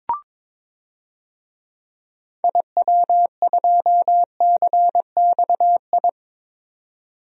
Morse code, heard as a keyed tone, about 22 wpm, 700 Hz. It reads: IW2CXI
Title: DXCC-100 Call Signs - 6 Characters - 22wpm